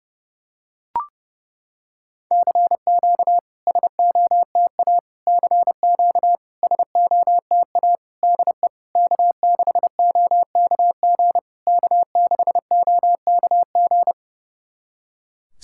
Morse code, heard as a keyed tone, about 30 wpm, 700 Hz. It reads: CQ SOTA CQ SOTA DE K6OKG K6OKG
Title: SOTA QSO Practice 30wpm